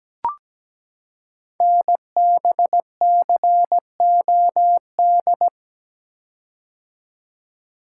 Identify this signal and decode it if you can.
Morse code, heard as a keyed tone, about 17 wpm, 700 Hz. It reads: NBCOD
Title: Code Groups - 5 Letters - 17wpm